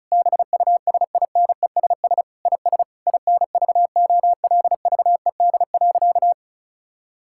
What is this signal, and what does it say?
Morse code, heard as a keyed tone, about 34 wpm, 700 Hz. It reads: BUSINESS IS INVOLVED.
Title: Sentences from Top 1000 Words - Review Part 5 - 35wpm